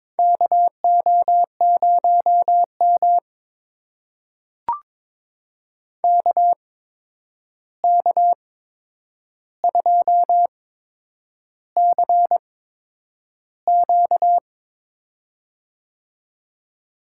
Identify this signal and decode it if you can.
Morse code, heard as a keyed tone, about 22 wpm, spaced as for 9 wpm, 700 Hz. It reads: KO0M KK2CQ
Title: USA Call Signs - 8x Character Spacing - 22wpm